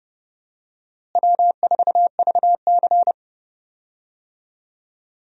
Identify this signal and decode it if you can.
Morse code, heard as a keyed tone, about 30 wpm, 700 Hz. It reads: W4VC
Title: USA Call Signs - 4 Characters - Repeated 3x - 30wpm